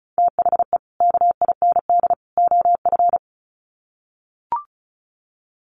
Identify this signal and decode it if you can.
Morse code, heard as a keyed tone, about 34 wpm, 700 Hz. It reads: THE KIND OF
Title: Sets of 3 Words 35wpm